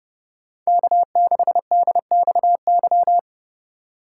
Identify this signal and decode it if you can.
Morse code, heard as a keyed tone, about 37 wpm, 700 Hz. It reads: K6DXY
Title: Speed Racing - USA 1x3 Call Signs 30wpm